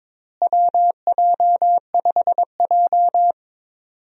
Morse code, heard as a keyed tone, about 22 wpm, 700 Hz. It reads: WJ5J
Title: USA Call Signs - 4 Characters - Repeated 2x - 22wpm